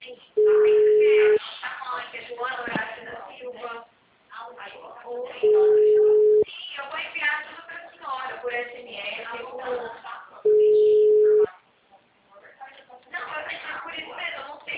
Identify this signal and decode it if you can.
Ringback tone (Europe, 425 Hz), with speech.
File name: out- -2015- - - .24645